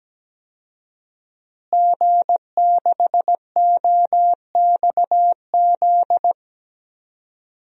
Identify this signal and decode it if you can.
Morse code, heard as a keyed tone, about 17 wpm, 700 Hz. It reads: G6OXZ
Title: DXCC-100 Call Signs - 5 Characters - Repeated 3x - 17wpm